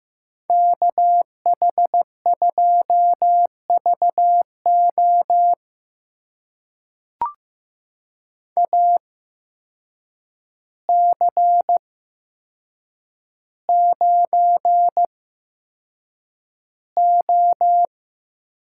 Morse code, heard as a keyed tone, about 15 wpm, spaced as for 7 wpm, 700 Hz. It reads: KH2VO AC9O
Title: USA Call Signs - 5 Characters - 8x Character Spacing - 15wpm